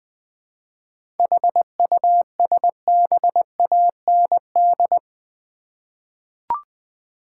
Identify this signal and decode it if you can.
Morse code, heard as a keyed tone, about 20 wpm, 700 Hz. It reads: HUSBAND
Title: Seven Letter Words - 20wpm